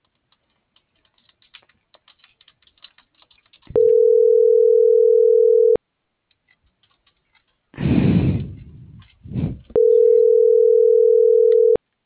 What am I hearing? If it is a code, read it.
Ringback tone (North America), 440 + 480 Hz.